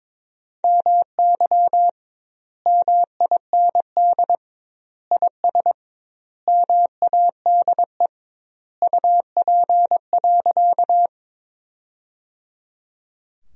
Morse code, heard as a keyed tone, about 22 wpm, 700 Hz. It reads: MY MIND IS MADE UP.
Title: 2x Word-Spacing - Sentences from Top 500 Words - 22wpm